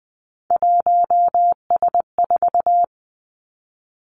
Morse code, heard as a keyed tone, about 20 wpm, 700 Hz. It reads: 1S4
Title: Three Challenging Letter-Number Combinations 20wpm